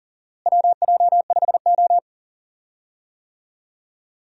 Morse code, heard as a keyed tone, about 40 wpm, 700 Hz. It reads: WJ5O
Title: USA Call Signs - 4 Characters - Repeated 3x - 40wpm